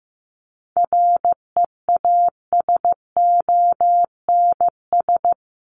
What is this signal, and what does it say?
Morse code, heard as a keyed tone, about 15 wpm, 700 Hz. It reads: REASONS
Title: ES:S Suffix - Top 100 Words with Base Words 15wpm